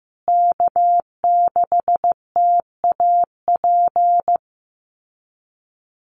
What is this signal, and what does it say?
Morse code, heard as a keyed tone, about 15 wpm, 700 Hz. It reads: K6TAP
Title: USA Call Signs Repeated 2x 15wpm